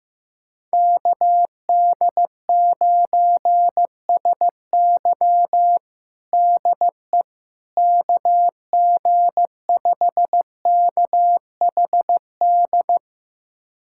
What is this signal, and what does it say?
Morse code, heard as a keyed tone, about 16 wpm, 700 Hz. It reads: KD9SY DE KG5KHD
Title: USA Callsign DE Callsign 15wpm